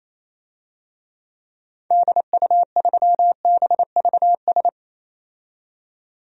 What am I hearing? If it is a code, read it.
Morse code, heard as a keyed tone, about 28 wpm, 700 Hz. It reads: DU3BVS
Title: DXCC-100 Call Signs - 6 Characters - 28wpm